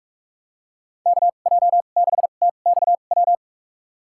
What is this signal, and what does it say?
Morse code, heard as a keyed tone, about 46 wpm, spaced as for 33 wpm, 700 Hz. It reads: KJ6TXW
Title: USA Call Signs - 6 Characters - 2x Character Spacing - 45wpm